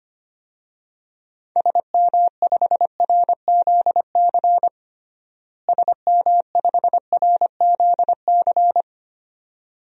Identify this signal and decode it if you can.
Morse code, heard as a keyed tone, about 25 wpm, 700 Hz. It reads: SM5RZC SM5RZC
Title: DXCC-100 Call Signs - 6 Characters - Repeated 2x - 25wpm